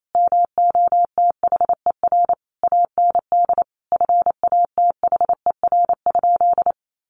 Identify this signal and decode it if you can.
Morse code, heard as a keyed tone, about 28 wpm, 700 Hz. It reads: MOTHER AND FATHER?
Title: Sentences from Top 1200 Words 28wpm